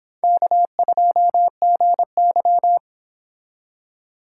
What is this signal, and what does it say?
Morse code, heard as a keyed tone, about 25 wpm, 700 Hz. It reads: K2GY